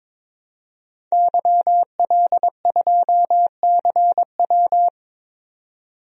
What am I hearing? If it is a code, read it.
Morse code, heard as a keyed tone, about 22 wpm, 700 Hz. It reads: YL2CW
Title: DXCC-100 Call Signs - 5 Characters - 22wpm